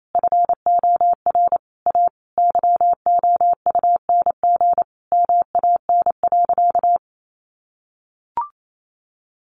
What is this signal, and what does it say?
Morse code, heard as a keyed tone, about 28 wpm, 700 Hz. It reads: FOR A YOUNG MAN.
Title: Sentences from Top 800 Words 28wpm